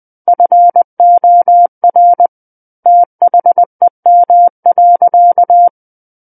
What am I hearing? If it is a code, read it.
Morse code, heard as a keyed tone, about 20 wpm, 700 Hz. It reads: FOR THEM.
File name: 20 WPM